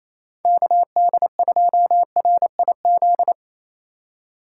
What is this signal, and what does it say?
Morse code, heard as a keyed tone, about 34 wpm, 700 Hz. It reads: KD2RIZ
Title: Speed Racing - USA 2x3 Call Signs 28wpm